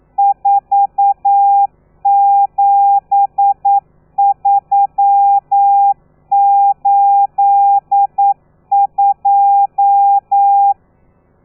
Morse code, heard as a keyed tone, about 9 wpm, 800 Hz. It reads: 47382
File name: telegrafia / i2tfj / Lez13 40